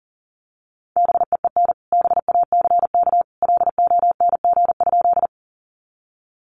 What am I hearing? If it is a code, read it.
Morse code, heard as a keyed tone, about 40 wpm, 700 Hz. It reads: BEEN BACK LONG?
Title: Sentences from Top 200 Words 40wpm